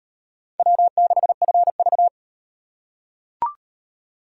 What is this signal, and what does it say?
Morse code, heard as a keyed tone, about 35 wpm, 700 Hz. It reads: W6FV